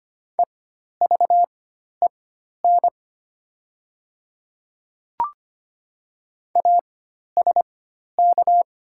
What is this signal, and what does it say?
Morse code, heard as a keyed tone, about 25 wpm, spaced as for 12 wpm, 700 Hz. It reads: EVEN ASK
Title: 4x Character Spacing - Top 300 Words - 25wpm